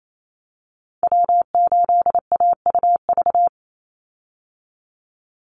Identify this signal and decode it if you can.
Morse code, heard as a keyed tone, about 28 wpm, 700 Hz. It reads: W8AUV
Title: USA Call Signs 28wpm